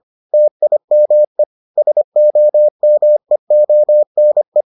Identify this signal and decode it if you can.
Morse code, heard as a keyed tone, about 25 wpm, 600 Hz. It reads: TIME SOMEONE